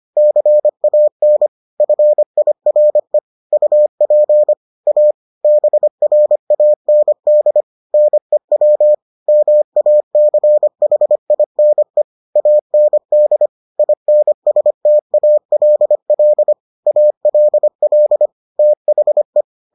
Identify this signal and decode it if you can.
Morse code, heard as a keyed tone, about 25 wpm, 600 Hz. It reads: CAN FIRE UP A BRAND NEW MACHINE AND INSTALL ALL THE